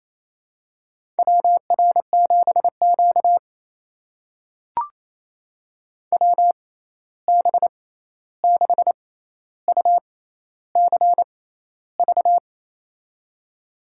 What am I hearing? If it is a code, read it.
Morse code, heard as a keyed tone, about 28 wpm, spaced as for 13 wpm, 700 Hz. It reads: WR7Q WB6UCV
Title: USA Call Signs - 6x Character Spacing - 28wpm